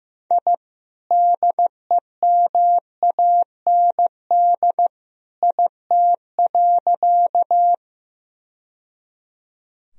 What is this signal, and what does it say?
Morse code, heard as a keyed tone, about 15 wpm, 700 Hz. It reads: I DEMAND IT.
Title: Sentences from Top 1000 Words - Review Part 4 - 15wpm